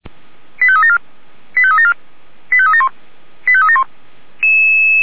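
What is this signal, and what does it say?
Five-tone selective call (ZVEI), heard as quick runs of five tones, tones of about 70 ms. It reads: 85374 85374 85371 85371
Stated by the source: ZVEI five-tone selective call received by SDR.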